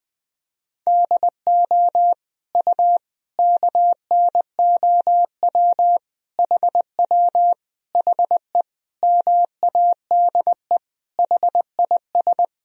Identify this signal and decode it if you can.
Morse code, heard as a keyed tone, about 21 wpm, 700 Hz. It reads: DO U KNOW HW HE MADE HIS
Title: Sentences from Top 300 Words and Abbreviations 20wpm